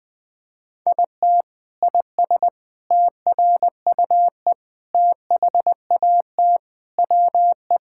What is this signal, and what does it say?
Morse code, heard as a keyed tone, about 21 wpm, 700 Hz. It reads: IT IS TRUE THAT WE
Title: Sentences from Top 1600 Words 20wpm